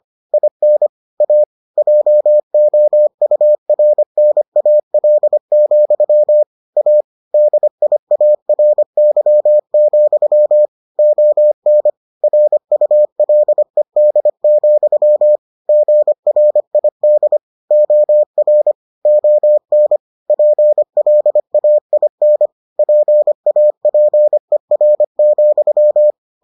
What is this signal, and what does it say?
Morse code, heard as a keyed tone, about 25 wpm, 600 Hz. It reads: IN A JOURNAL, A DIARY, ON RULED, GRID OR ON PLAIN PAPER,